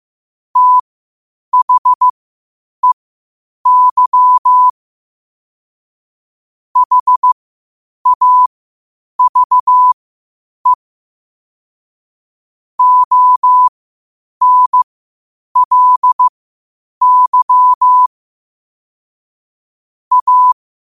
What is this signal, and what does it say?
Morse code, heard as a keyed tone, about 15 wpm, spaced as for 8 wpm, 1000 Hz. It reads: THEY HAVE ONLY A
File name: 8wpm